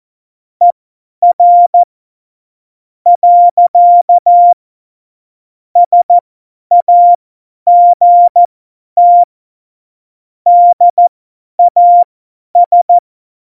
Morse code, heard as a keyed tone, about 14 wpm, spaced as for 9 wpm, 700 Hz. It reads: ER. SAGT DAS
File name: Morsekurs / Witze / Witz 25. Dezember, 50 F70